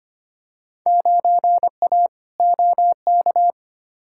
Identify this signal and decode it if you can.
Morse code, heard as a keyed tone, about 25 wpm, 700 Hz. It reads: 9A OK
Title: Field Day Exchanges Encore - 25wpm